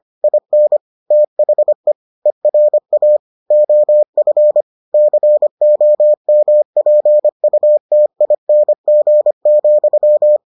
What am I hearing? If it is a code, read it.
Morse code, heard as a keyed tone, about 25 wpm, 600 Hz. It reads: IN THE ERA OF COMPUTING,